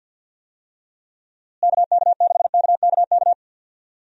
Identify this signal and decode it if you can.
Morse code, heard as a keyed tone, about 52 wpm, 700 Hz. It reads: KK6KKK
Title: USA Call Signs - 6 Characters - Repeated 3x - 50wpm